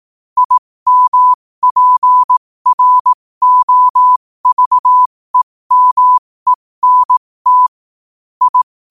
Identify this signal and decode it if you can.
Morse code, heard as a keyed tone, about 18 wpm, spaced as for 15 wpm, 1000 Hz. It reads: IMPROVEMENT I
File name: qotdfiles / 15wpm